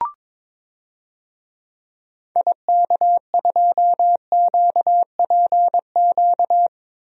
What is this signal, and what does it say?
Morse code, heard as a keyed tone, about 21 wpm, 700 Hz. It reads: IK2QPQ
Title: DXCC-100 Call Signs - 6 Characters - Repeated 2x - 22wpm